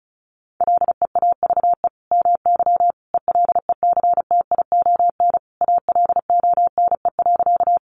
Morse code, heard as a keyed tone, about 34 wpm, 700 Hz. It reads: LEAVE MY ELECTION ALONE.
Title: Sentences from Top 1500 Words 35wpm